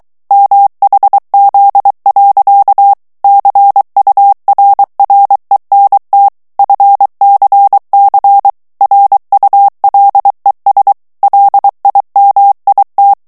Morse code, heard as a keyed tone, about 23 wpm, 800 Hz. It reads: MHZ. CURRENT FCC RULES LIMIT
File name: ANGLAIS texte clair 3 vitesse 22 mots minute